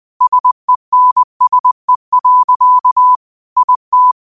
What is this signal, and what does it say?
Morse code, heard as a keyed tone, about 21 wpm, 1000 Hz. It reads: SENSE. IT